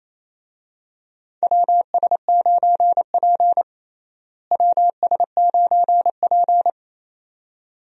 Morse code, heard as a keyed tone, about 28 wpm, 700 Hz. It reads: WS9P WS9P
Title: USA Call Signs - 4 Characters - Repeated 2x - 28wpm